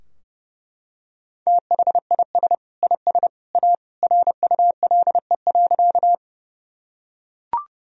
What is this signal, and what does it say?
Morse code, heard as a keyed tone, about 32 wpm, 700 Hz. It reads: THIS IS A RULE.
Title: Sentences from Top 1000 Words - Review Part 2 - 30wpm